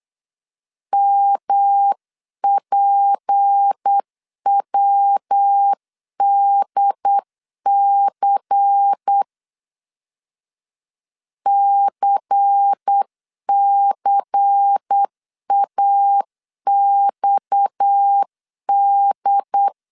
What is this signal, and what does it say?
Morse code, heard as a keyed tone, about 8 wpm, 800 Hz. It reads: MPWDC CCAXD
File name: CW / DET / 1 6 1500C S C 8wpm 5219